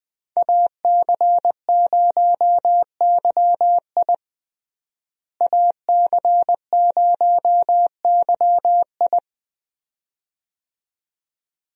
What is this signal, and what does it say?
Morse code, heard as a keyed tone, about 20 wpm, 700 Hz. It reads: AC0YI AC0YI
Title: USA Call Signs - 5 Characters - Repeated 2x - 20wpm